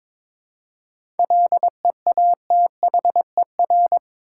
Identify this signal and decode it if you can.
Morse code, heard as a keyed tone, about 22 wpm, 700 Hz. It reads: LEATHER
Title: Seven Letter Words - 22wpm